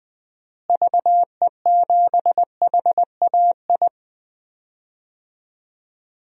Morse code, heard as a keyed tone, about 21 wpm, 700 Hz. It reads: VE7HAI
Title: DXCC-100 Call Signs - 6 Characters - Repeated 2x - 20wpm